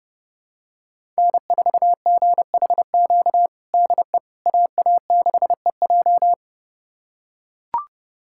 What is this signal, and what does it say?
Morse code, heard as a keyed tone, about 30 wpm, 700 Hz. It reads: N4GHQ DE AA6EJ
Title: USA Callsign DE Callsign - 5 Characters - 30wpm